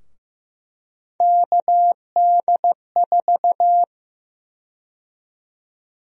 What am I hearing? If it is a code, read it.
Morse code, heard as a keyed tone, about 15 wpm, 700 Hz. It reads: KD4
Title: USA Call Sign Prefixes - 15wpm